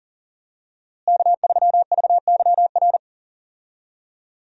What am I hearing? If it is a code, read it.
Morse code, heard as a keyed tone, about 40 wpm, 700 Hz. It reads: K3VYR
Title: USA Call Signs - 5 Characters - Repeated 3x - 40wpm